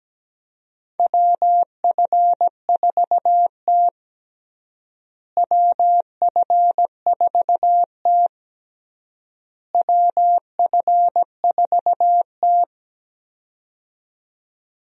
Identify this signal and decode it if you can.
Morse code, heard as a keyed tone, about 17 wpm, 700 Hz. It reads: WF4T WF4T WF4T
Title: USA Call Signs - 4 Characters - Repeated 3x - 17wpm